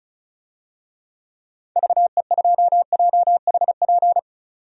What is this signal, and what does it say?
Morse code, heard as a keyed tone, about 36 wpm, 700 Hz. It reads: VE2JHP
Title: DXCC-100 Call Signs - 6 Characters - 35wpm